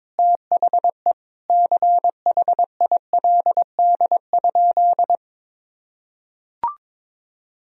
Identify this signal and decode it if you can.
Morse code, heard as a keyed tone, about 21 wpm, 700 Hz. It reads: THE CHILD?
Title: Sentences from Top 1000 Words - Review Part 1 - 22wpm